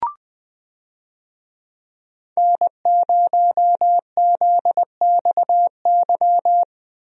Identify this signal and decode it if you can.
Morse code, heard as a keyed tone, about 20 wpm, 700 Hz. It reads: N0ZXY
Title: USA Call Signs - 5 Characters - Repeated 3x - 20wpm